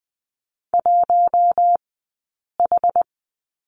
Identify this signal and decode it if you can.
Morse code, heard as a keyed tone, about 20 wpm, 700 Hz. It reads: 1 H
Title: 066 Lesson 12d - Letter d - Two Characters 20wpm